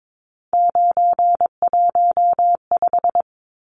Morse code, heard as a keyed tone, about 22 wpm, 700 Hz. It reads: 915
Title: Numbers - 3-Digit 22wpm